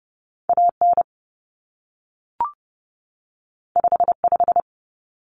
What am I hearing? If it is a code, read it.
Morse code, heard as a keyed tone, about 30 wpm, 700 Hz. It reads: AN 55
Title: Common Cut Numbers - 2-Digit 30wpm